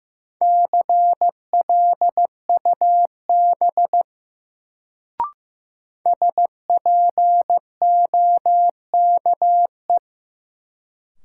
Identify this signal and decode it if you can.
Morse code, heard as a keyed tone, about 15 wpm, 700 Hz. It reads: CLUB SPOKE
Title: Sentences from Top 1200 Words - New Word Drill - 15wpm